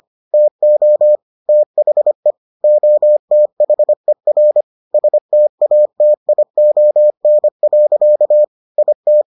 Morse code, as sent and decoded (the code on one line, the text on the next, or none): - --- / - .... / --- - .... .-. / ... - .- - .. --- -. .-.-.- / .. -
TO THE OTHER STATION. IT